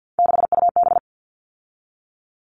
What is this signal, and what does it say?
Morse code, heard as a keyed tone, about 50 wpm, 700 Hz. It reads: 6UB
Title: All Cut Numbers - 3-Digit 50wpm